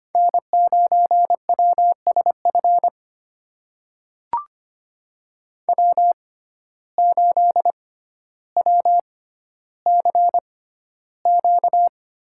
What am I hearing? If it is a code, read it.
Morse code, heard as a keyed tone, about 25 wpm, spaced as for 13 wpm, 700 Hz. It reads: N9WSF W8WCQ
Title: USA Call Signs - 6x Character Spacing - 25wpm